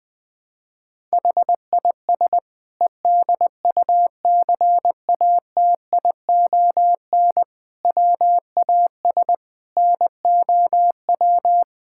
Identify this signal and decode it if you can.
Morse code, heard as a keyed tone, about 21 wpm, 700 Hz. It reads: HIS EDUCATION WAS NOW
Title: Sentences from Top 1000 Words - Review Part 6 - 20wpm